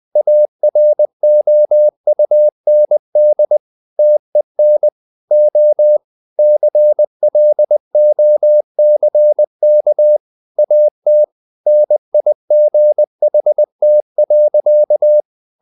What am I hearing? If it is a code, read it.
Morse code, heard as a keyed tone, about 20 wpm, 600 Hz. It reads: AROUND TEN O CLOCK AT NIGHT.
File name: the old man and the sea part 2 20wpm